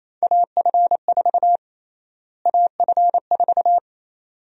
Morse code, heard as a keyed tone, about 28 wpm, 700 Hz. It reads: AF4 AF4
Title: USA Call Sign Prefixes - Repeated 2x - 28wpm